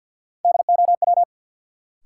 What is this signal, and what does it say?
Morse code, heard as a keyed tone, about 52 wpm, 700 Hz. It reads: DOW
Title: Sets of 5 Words - New Word Drill - 50wpm